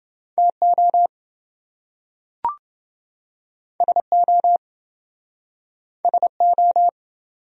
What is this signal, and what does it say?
Morse code, heard as a keyed tone, about 28 wpm, 700 Hz. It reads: TO SO SO